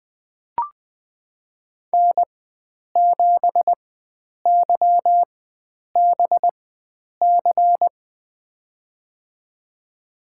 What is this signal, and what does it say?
Morse code, heard as a keyed tone, about 20 wpm, spaced as for 9 wpm, 700 Hz. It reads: N7YBC
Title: USA Call Signs - 5 Characters - 4x Character Spacing - 20wpm